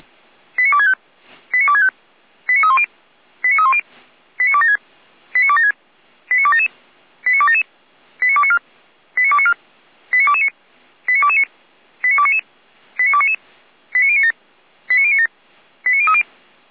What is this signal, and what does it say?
Five-tone selective call (ZVEI), heard as quick runs of five tones, tones of about 70 ms. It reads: 89265 89265 89310 89310 89276 89276 89277 89277 89294 89294 89229 89229 89299 89299 89097 89097 89030
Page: ZVEI